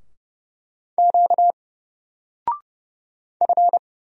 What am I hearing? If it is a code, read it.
Morse code, heard as a keyed tone, about 31 wpm, 700 Hz. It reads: Q F
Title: Single Letter-Number Alt 30wpm